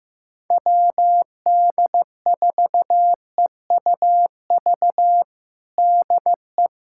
Morse code, heard as a keyed tone, about 15 wpm, 700 Hz. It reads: WD4EUV DE
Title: USA Callsign DE Callsign - 6 Characters - 15wpm